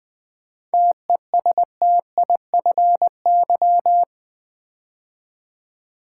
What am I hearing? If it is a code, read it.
Morse code, heard as a keyed tone, about 20 wpm, 690 Hz. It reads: TESTIFY
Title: 7 Letter Words - First 3 Letters Given - 20wpm